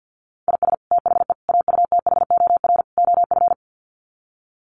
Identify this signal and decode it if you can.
Morse code, heard as a keyed tone, about 50 wpm, 700 Hz. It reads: IS THE AUTHOR OF
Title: Sets of 4 Words 50wpm